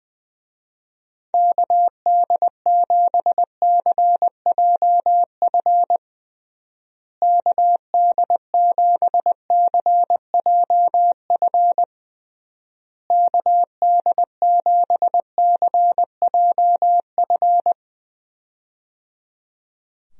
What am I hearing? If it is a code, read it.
Morse code, heard as a keyed tone, about 20 wpm, 700 Hz. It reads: KD7CJF KD7CJF KD7CJF
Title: USA Call Signs - 6 Characters - Repeated 3x - 20wpm